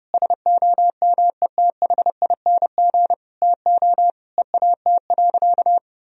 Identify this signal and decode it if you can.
Morse code, heard as a keyed tone, about 30 wpm, 700 Hz. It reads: SOMETHING TO EAT.
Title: Sentences from Top 1000 Words - Review Part 3 - 30wpm